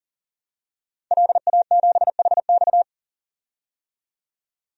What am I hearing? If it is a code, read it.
Morse code, heard as a keyed tone, about 40 wpm, 700 Hz. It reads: LA7HX
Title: DXCC-100 Call Signs - 5 Characters - 40wpm